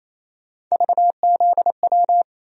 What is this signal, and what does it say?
Morse code, heard as a keyed tone, about 28 wpm, 700 Hz. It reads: VZW
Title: USA Call Sign Postfixes - 3 Letters 28wpm